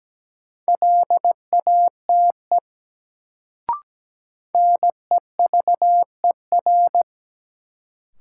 Morse code, heard as a keyed tone, about 17 wpm, 700 Hz. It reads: LATE NEVER
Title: Sentences from Top 300 Words - New Word Drill - 17wpm